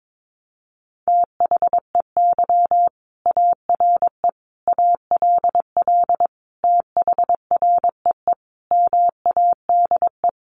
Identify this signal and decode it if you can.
Morse code, heard as a keyed tone, about 22 wpm, 700 Hz. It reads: THEY ARE ALL THREE MADE
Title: Sentences from Top 1500 Words 22wpm